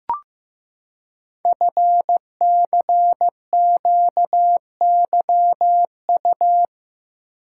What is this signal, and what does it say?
Morse code, heard as a keyed tone, about 15 wpm, 700 Hz. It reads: FCQYU
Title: Code Groups - 5 Letters - 15wpm